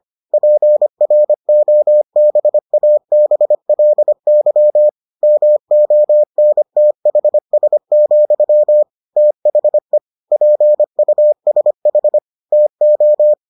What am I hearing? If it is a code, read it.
Morse code, heard as a keyed tone, about 25 wpm, 600 Hz. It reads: PROBABLY MONTHS, THE PUSH TO